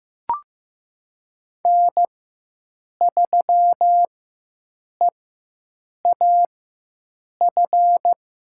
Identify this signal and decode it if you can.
Morse code, heard as a keyed tone, about 16 wpm, spaced as for 7 wpm, 700 Hz. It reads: N3EAF
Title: USA Call Signs - 5 Characters - 4x Character Spacing - 15wpm